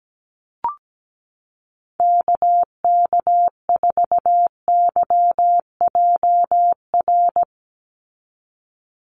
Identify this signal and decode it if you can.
Morse code, heard as a keyed tone, about 17 wpm, 700 Hz. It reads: KK4YJR
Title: USA Call Signs 17wpm